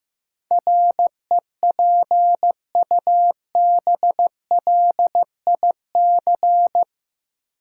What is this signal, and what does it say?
Morse code, heard as a keyed tone, about 15 wpm, 700 Hz. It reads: REPUBLIC
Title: Sentences from Top 1500 Words - New Word Drill Alt - 15wpm